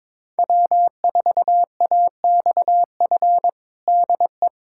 Morse code, heard as a keyed tone, about 22 wpm, 700 Hz. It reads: W4AXF DE
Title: USA Callsign DE Callsign - 5 Characters - 22wpm